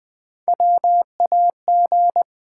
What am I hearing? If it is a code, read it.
Morse code, heard as a keyed tone, about 20 wpm, 700 Hz. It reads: WAG
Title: Build A Word - 1 Letter At A Time - 3 to 5-Letter Words - 20wpm